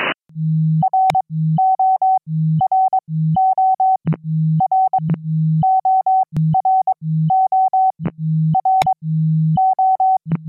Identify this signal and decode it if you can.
Morse code, heard as a keyed tone, about 22 wpm, spaced as for 13 wpm, 760 Hz. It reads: RO RO RO RO RO
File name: QSO-PA3FXB-DJ5AR